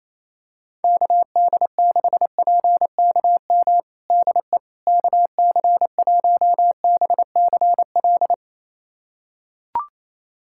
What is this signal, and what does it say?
Morse code, heard as a keyed tone, about 28 wpm, 700 Hz. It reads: KD6PKM DE KC1BCL
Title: USA Callsign DE Callsign 28wpm